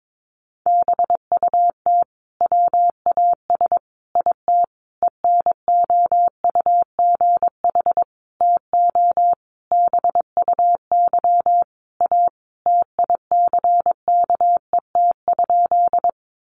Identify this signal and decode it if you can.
Morse code, heard as a keyed tone, about 22 wpm, 700 Hz. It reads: BUT WAS IT ENOUGH TO BUY A TICKET?
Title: Sentences from Top 1400 Words 22wpm